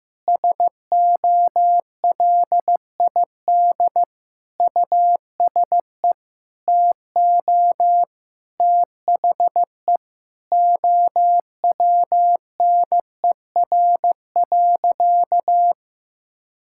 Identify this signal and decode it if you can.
Morse code, heard as a keyed tone, about 15 wpm, 700 Hz. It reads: SOLID USE TO THE OWNER.
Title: Sentences from Top 1600 Words 15wpm